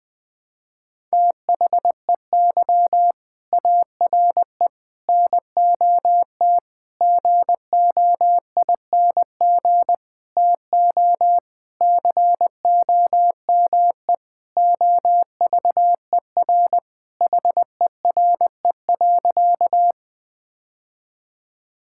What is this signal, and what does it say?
Morse code, heard as a keyed tone, about 21 wpm, 700 Hz. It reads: THEY ARE NOT GOING TO COME OVER HERE.
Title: Sentences with 4-Word Phrases - 20wpm